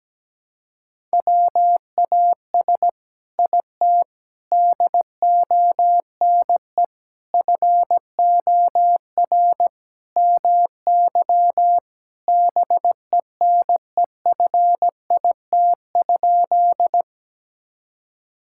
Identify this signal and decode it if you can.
Morse code, heard as a keyed tone, about 17 wpm, 700 Hz. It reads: WAS IT DONE FOR MY BENEFIT?
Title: Sentences from Top 1000 Words - Review Part 6 - 17wpm